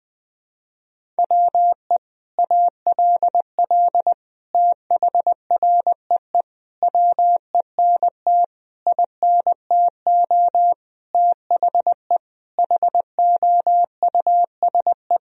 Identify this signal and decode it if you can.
Morse code, heard as a keyed tone, about 20 wpm, 700 Hz. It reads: WE ALL THREE WENT INTO THE HOUSE
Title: Sentences from Top 1000 Words - Review Part 1 - 20wpm